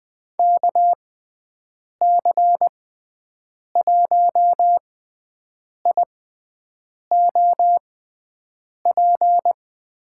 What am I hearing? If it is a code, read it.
Morse code, heard as a keyed tone, about 20 wpm, spaced as for 7 wpm, 700 Hz. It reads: KC1IOP
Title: USA Call Signs - 6 Characters - 6x Character Spacing - 20wpm